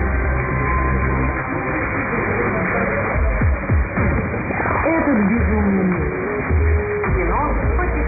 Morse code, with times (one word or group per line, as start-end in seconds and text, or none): none